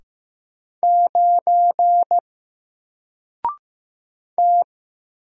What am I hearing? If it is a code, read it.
Morse code, heard as a keyed tone, about 15 wpm, 700 Hz. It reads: 9 T
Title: Single Letter-Number 15wpm